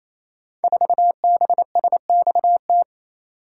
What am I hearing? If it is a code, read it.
Morse code, heard as a keyed tone, about 27 wpm, 700 Hz. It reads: 4BSXT
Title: Code Groups - 5 Letter:Numbers - 28wpm